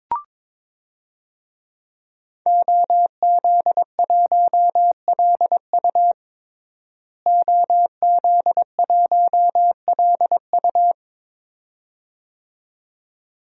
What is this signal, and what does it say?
Morse code, heard as a keyed tone, about 21 wpm, 700 Hz. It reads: OZ1LU OZ1LU
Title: DXCC-100 Call Signs - 5 Characters - Repeated 2x - 22wpm